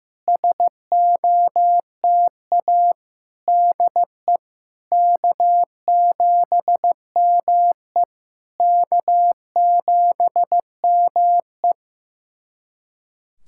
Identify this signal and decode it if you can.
Morse code, heard as a keyed tone, about 15 wpm, 700 Hz. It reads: SOTA DE K7ME K7ME
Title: SOTA QSO Practice 15wpm